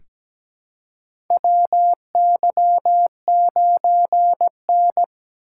Morse code, heard as a keyed tone, about 17 wpm, 700 Hz. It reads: WY9N
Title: USA Call Signs - 4 Characters - 2x Character Spacing - 17wpm